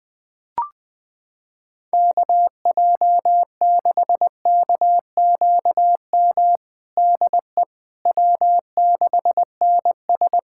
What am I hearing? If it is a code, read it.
Morse code, heard as a keyed tone, about 20 wpm, 700 Hz. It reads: KJ6KQM DE W6NS
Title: Practice QSO - Exchange 4 - Ending - 20wpm